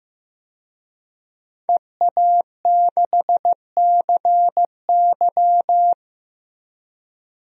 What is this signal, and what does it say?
Morse code, heard as a keyed tone, about 15 wpm, 700 Hz. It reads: EA6CY
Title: DXCC-100 Call Signs - 5 Characters - Repeated 2x - 15wpm